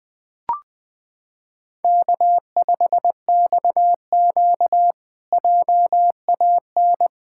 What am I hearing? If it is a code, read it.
Morse code, heard as a keyed tone, about 20 wpm, 700 Hz. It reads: K5XQ JAN
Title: K1USN SST QSO Practice - 20wpm